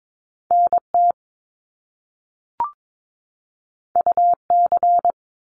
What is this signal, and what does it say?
Morse code, heard as a keyed tone, about 22 wpm, 700 Hz. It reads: NT UC
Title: Top Two Letter Combinations in Words 22wpm